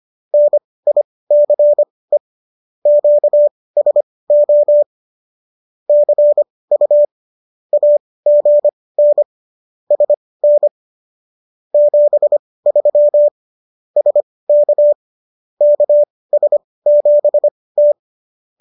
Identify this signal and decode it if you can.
Morse code, heard as a keyed tone, about 25 wpm, spaced as for 18 wpm, 590 Hz. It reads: NICE QSO CU AGN SN 73 SK KS7T